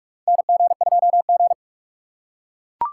Morse code, heard as a keyed tone, about 42 wpm, 700 Hz. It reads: NG2G
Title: USA Call Signs - 4 Characters - 2x Character Spacing - 45wpm